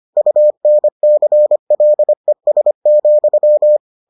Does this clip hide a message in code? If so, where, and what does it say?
Yes: Morse code from 0.2 to 3.8 s, heard as a keyed tone, about 25 wpm, 600 Hz. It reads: UNCLES,